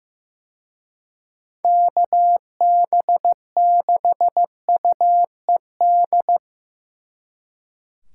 Morse code, heard as a keyed tone, about 17 wpm, 700 Hz. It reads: KB6UED
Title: Speed Racing - USA 2x3 Call Signs 15wpm